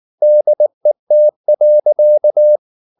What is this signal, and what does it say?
Morse code, heard as a keyed tone, about 19 wpm, 600 Hz. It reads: DET.
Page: Ferie 19 wpm